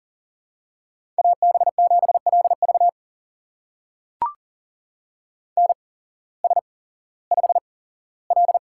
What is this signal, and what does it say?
Morse code, heard as a keyed tone, about 41 wpm, spaced as for 19 wpm, 700 Hz. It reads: AB7LV NS5L
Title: USA Call Signs - 8x Character Spacing - 40wpm